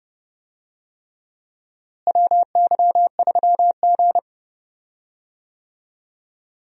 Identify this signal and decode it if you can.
Morse code, heard as a keyed tone, about 30 wpm, 700 Hz. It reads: WY3G
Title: USA Call Signs - 4 Characters - Repeated 3x - 30wpm